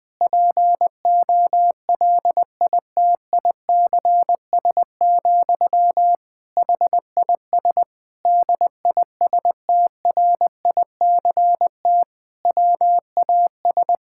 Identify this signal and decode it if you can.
Morse code, heard as a keyed tone, about 20 wpm, 700 Hz. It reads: POLITICS, HIS DISTRICT WAS
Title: Sentences from Top 1600 Words 20wpm